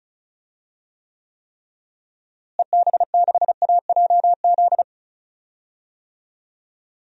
Morse code, heard as a keyed tone, about 36 wpm, 700 Hz. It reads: EB6AJZ
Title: DXCC-100 Call Signs - 6 Characters - Repeated 3x - 35wpm